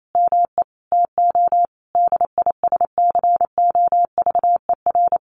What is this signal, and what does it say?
Morse code, heard as a keyed tone, about 28 wpm, 700 Hz. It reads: ME TO DISCOVER
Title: Sentences from Top 1200 Words 28wpm